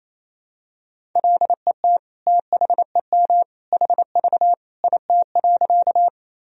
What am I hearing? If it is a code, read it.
Morse code, heard as a keyed tone, about 28 wpm, 700 Hz. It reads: LET THEM HV IT.
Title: Sentences from Top 300 Words and Abbreviations 28wpm